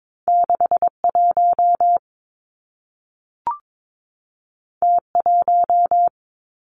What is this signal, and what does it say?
Morse code, heard as a keyed tone, about 22 wpm, 700 Hz. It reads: 6 1 T 1
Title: Two Challenging Letter-Number Combinations 22wpm